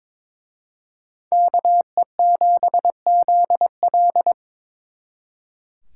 Morse code, heard as a keyed tone, about 26 wpm, 700 Hz. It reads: KE7ZL
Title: Speed Racing - USA 2x2 Call Signs 22wpm